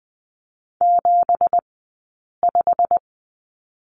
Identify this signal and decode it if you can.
Morse code, heard as a keyed tone, about 20 wpm, spaced as for 8 wpm, 700 Hz. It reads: 75
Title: 171 Lesson 28d - Number 9 - Two Characters 20wpm